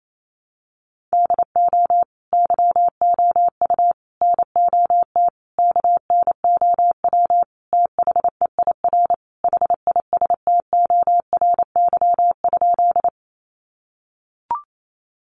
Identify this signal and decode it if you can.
Morse code, heard as a keyed tone, about 28 wpm, 700 Hz. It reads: DO YOU NOT KNOW THEIR HISTORY?
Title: Sentences from Top 600 Words 28wpm